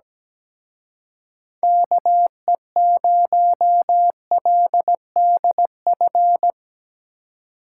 Morse code, heard as a keyed tone, about 19 wpm, 700 Hz. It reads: KE0LDF
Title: Speed Racing - USA 2x3 Call Signs 17wpm